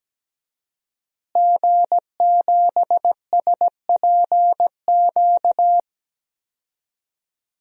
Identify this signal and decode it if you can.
Morse code, heard as a keyed tone, about 17 wpm, 700 Hz. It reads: G7SPQ
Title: DXCC-100 Call Signs - 5 Characters - Repeated 2x - 17wpm